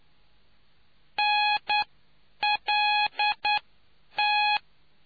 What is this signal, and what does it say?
Morse code, heard as a keyed tone, about 9 wpm, spaced as for 8 wpm, 840 Hz. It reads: NLT